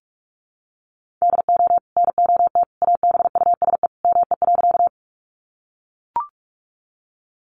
Morse code, heard as a keyed tone, about 44 wpm, 700 Hz. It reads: DO NOT ABUSE ME.
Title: Sentences from Top 1300 Words 45wpm